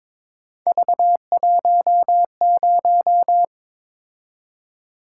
Morse code, heard as a keyed tone, about 22 wpm, 700 Hz. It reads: V10
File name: practice / 22wpm / Three Letter-Number Combinations - 22wpm